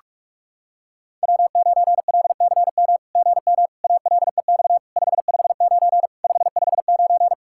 Morse code, heard as a keyed tone, about 44 wpm, 700 Hz. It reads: W9LCM GM ALEX 559 559
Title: SOTA QSO Practice 45wpm